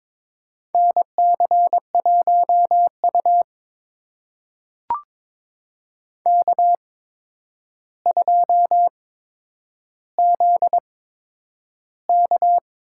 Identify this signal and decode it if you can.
Morse code, heard as a keyed tone, about 22 wpm, spaced as for 9 wpm, 700 Hz. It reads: NC1U K2ZK
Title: USA Call Signs - 4 Characters - 8x Character Spacing - 22wpm